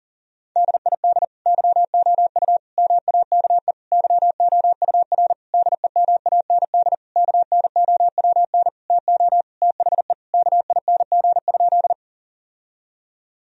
Morse code, heard as a keyed tone, about 40 wpm, 700 Hz. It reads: DID YOU MAKE YOUR DEMAND KNOWN TO THE KING?
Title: Sentences from Top 1000 Words - Review Part 4 - 40wpm